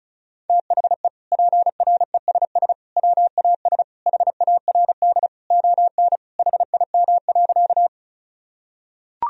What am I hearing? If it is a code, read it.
Morse code, heard as a keyed tone, about 36 wpm, 700 Hz. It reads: THE PRESS WAS HARD ON HIM.
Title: Sentences from Top 1000 Words - Review Part 1 - 35wpm